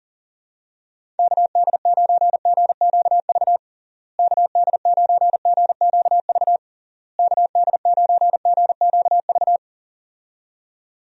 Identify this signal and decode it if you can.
Morse code, heard as a keyed tone, about 40 wpm, 700 Hz. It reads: KD9GQV KD9GQV KD9GQV
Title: USA Call Signs - 6 Characters - Repeated 3x - 40wpm